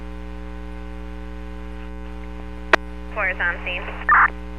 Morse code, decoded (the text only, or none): none